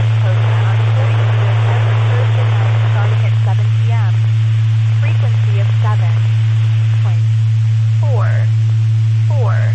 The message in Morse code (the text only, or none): none